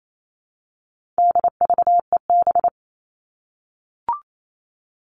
Morse code, heard as a keyed tone, about 28 wpm, 700 Hz. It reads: DVEB
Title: All Cut Numbers - 4-Digit 28wpm